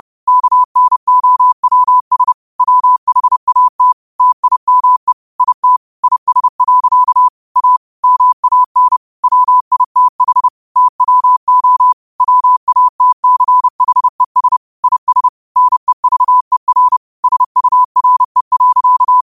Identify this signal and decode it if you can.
Morse code, heard as a keyed tone, about 30 wpm, 1000 Hz. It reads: KNOWS WHAT TIME IT IS. A MAN WITH TWO WATCHES IS NEVER SURE.
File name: 30wpm